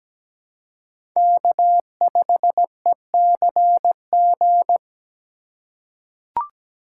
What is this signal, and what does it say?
Morse code, heard as a keyed tone, about 17 wpm, 700 Hz. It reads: K5ECG
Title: USA Call Signs - 5 Characters - 4x Character Spacing - 17wpm